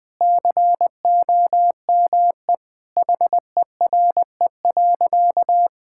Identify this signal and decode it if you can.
Morse code, heard as a keyed tone, about 20 wpm, 700 Hz. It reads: COME HERE.
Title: Sentences with 2-Word Phrases - 20wpm